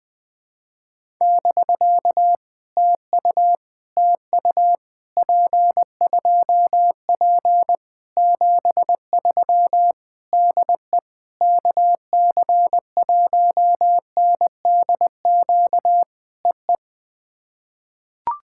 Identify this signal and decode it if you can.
Morse code, heard as a keyed tone, about 20 wpm, 700 Hz. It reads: <BK> TU TU P2P 73 DE KC1NDQ EE
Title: POTA QSO Practice - 20wpm